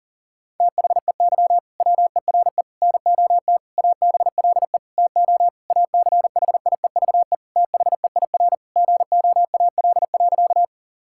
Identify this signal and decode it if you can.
Morse code, heard as a keyed tone, about 37 wpm, 700 Hz. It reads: THEY WERE NOT ABLE TO ACHIEVE THEIR GOAL.
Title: Sentences from Top 1000 Words - Review Part 6 - 40wpm